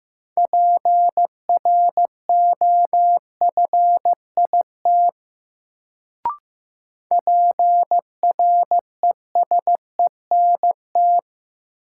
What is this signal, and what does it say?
Morse code, heard as a keyed tone, about 15 wpm, 700 Hz. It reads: PROFIT PRESENT
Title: Sentences from Top 800 Words - New Word Drill - 15wpm